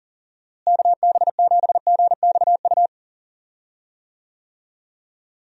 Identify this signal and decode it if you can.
Morse code, heard as a keyed tone, about 40 wpm, 700 Hz. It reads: KB7GXU
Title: USA Call Signs - 6 Characters - Repeated 3x - 40wpm